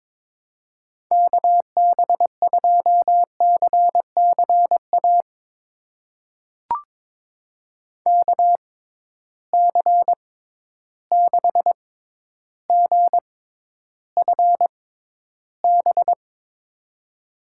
Morse code, heard as a keyed tone, about 22 wpm, spaced as for 11 wpm, 700 Hz. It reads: KB2CCA KC6GFB
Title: USA Call Signs - 6 Characters - 6x Character Spacing - 22wpm